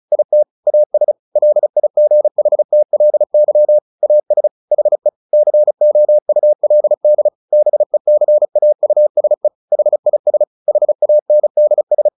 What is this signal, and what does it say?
Morse code, heard as a keyed tone, about 34 wpm, 600 Hz. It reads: IT AS LIGHTLY AS HE COULD BECAUSE HIS HANDS
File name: the old man and the sea 2 35wpm